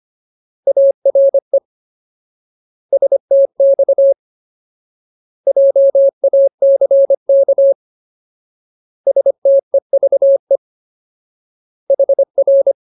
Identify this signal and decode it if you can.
Morse code, heard as a keyed tone, about 25 wpm, 550 Hz. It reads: ARE STX JACK STEVE HR
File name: Ultimate Word List / 25WPM Ultimate